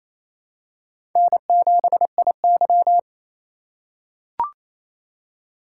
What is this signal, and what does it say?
Morse code, heard as a keyed tone, about 28 wpm, 700 Hz. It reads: N7IY
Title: USA Call Signs - 2x Character Spacing - 28wpm